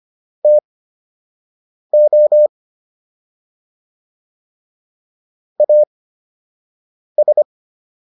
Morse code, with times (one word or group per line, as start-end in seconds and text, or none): 0.4-2.5 TO
5.6-7.4 AS